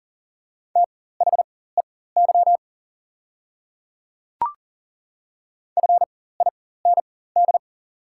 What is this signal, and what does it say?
Morse code, heard as a keyed tone, about 39 wpm, spaced as for 19 wpm, 690 Hz. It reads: THEY FIND
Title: 4x Character Spacing - Top 100 Words - 40wpm